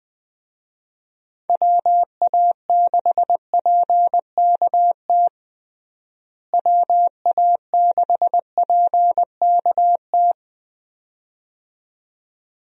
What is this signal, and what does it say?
Morse code, heard as a keyed tone, about 21 wpm, 700 Hz. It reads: WA6PKT WA6PKT
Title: USA Call Signs - 6 Characters - Repeated 2x - 20wpm